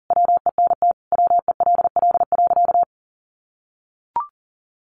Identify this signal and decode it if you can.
Morse code, heard as a keyed tone, about 40 wpm, 700 Hz. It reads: WENT WELL.
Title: Sentences from Top 800 Words 40wpm